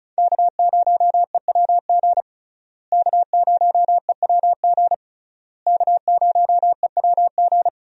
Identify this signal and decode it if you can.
Morse code, heard as a keyed tone, about 34 wpm, 700 Hz. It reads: K0EWG K0EWG K0EWG
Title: USA Call Signs - 5 Characters - Repeated 3x - 35wpm